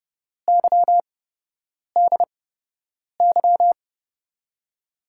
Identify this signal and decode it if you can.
Morse code, heard as a keyed tone, about 32 wpm, spaced as for 8 wpm, 700 Hz. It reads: YDY
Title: USA Call Signs - 8x Character Spacing - 30wpm